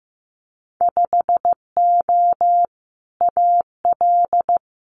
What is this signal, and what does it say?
Morse code, heard as a keyed tone, about 15 wpm, 700 Hz. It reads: 5O AL
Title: Winter Field Day Exchanges 15wpm